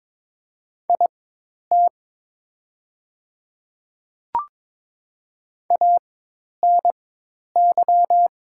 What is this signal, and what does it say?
Morse code, heard as a keyed tone, about 22 wpm, spaced as for 10 wpm, 700 Hz. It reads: IT ANY
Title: 4x Character Spacing - Top 300 Words - 22wpm